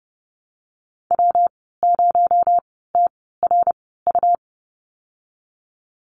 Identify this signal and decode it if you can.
Morse code, heard as a keyed tone, about 30 wpm, spaced as for 17 wpm, 700 Hz. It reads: W0TRU
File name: Lesson 37g - Letter x - Call Signs 30wpm